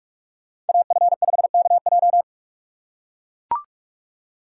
Morse code, heard as a keyed tone, about 42 wpm, 700 Hz. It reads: AF5KJ
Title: USA Call Signs - 5 Characters - 6x Character Spacing - 45wpm